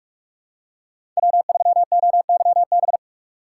Morse code, heard as a keyed tone, about 46 wpm, 700 Hz. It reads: W3OYB
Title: USA Call Signs - 5 Characters - 2x Character Spacing - 45wpm